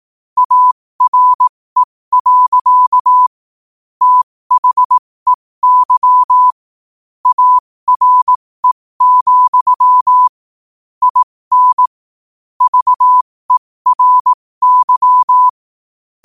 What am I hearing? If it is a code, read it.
Morse code, heard as a keyed tone, about 18 wpm, spaced as for 15 wpm, 1000 Hz. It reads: ARE. THEY ARE, IN VERY
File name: qotdfiles / 15wpm